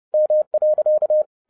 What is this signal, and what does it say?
Morse code, heard as a keyed tone, about 30 wpm, 610 Hz. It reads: M.
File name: 30wpm